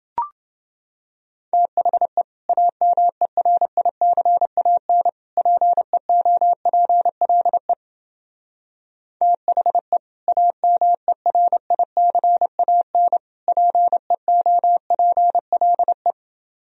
Morse code, heard as a keyed tone, about 30 wpm, 700 Hz. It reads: THE AMERICAN PEOPLE THE AMERICAN PEOPLE